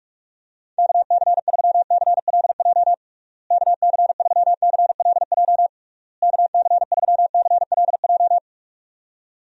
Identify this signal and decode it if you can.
Morse code, heard as a keyed tone, about 41 wpm, 700 Hz. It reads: KC3CLJ KC3CLJ KC3CLJ
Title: USA Call Signs - 6 Characters - Repeated 3x - 45wpm